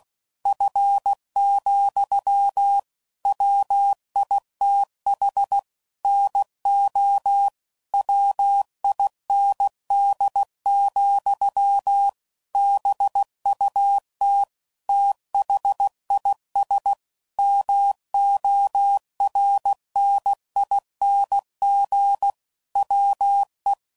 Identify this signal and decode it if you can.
Morse code, heard as a keyed tone, about 16 wpm, 800 Hz. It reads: F, WITH NO WIND, BUT THIS MORNING WE
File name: CW Code Groups @15WPM / Weather Sets / Weather 3 17